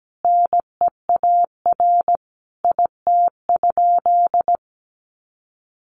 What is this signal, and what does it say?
Morse code, heard as a keyed tone, about 17 wpm, 700 Hz. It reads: NEAR IT?
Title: Sentences from Top 1400 Words 17wpm